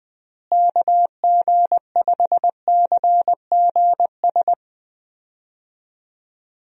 Morse code, heard as a keyed tone, about 20 wpm, 700 Hz. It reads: KG5CGS
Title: USA Call Signs - 6 Characters - Repeated 3x - 20wpm